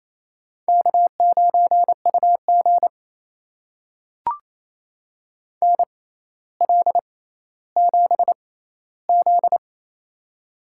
Morse code, heard as a keyed tone, about 28 wpm, spaced as for 14 wpm, 700 Hz. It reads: K9UG NL7Z
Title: USA Call Signs - 4 Characters - 6x Character Spacing - 28wpm